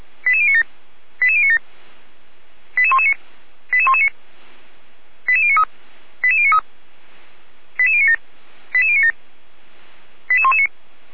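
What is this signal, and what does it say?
Five-tone selective call (ZVEI), heard as quick runs of five tones, tones of about 70 ms. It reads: 88097 88097 88119 88119 88093 88093 88097 88097 88119
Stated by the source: Received on SDR as ZVEI five-tone selective call; archived by feuerwehr-orlamuende.org (fire brigade Orlamünde).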